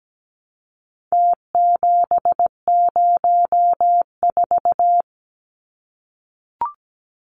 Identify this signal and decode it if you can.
Morse code, heard as a keyed tone, about 17 wpm, 700 Hz. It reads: T704
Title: Numbers - Contest Style - 4-Digit 17wpm